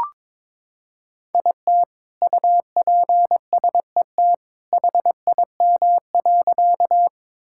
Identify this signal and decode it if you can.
Morse code, heard as a keyed tone, about 22 wpm, 700 Hz. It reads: IT UPSET HIM.
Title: Sentences from Top 1000 Words - Review Part 5 - 22wpm